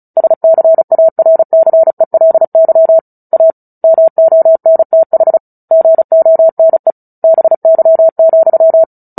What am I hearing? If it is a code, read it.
Morse code, heard as a keyed tone, about 34 wpm, 650 Hz. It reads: SCARCELY A MONTH GONE BY,
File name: Book Two / 09-Chapter 9 - 35 WPM